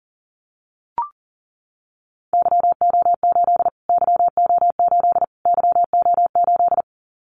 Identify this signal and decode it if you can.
Morse code, heard as a keyed tone, about 40 wpm, 700 Hz. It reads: YO8 YO8 YO8
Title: Callsign Prefixes for Top 25 DXCC - Repeated 3x - 40wpm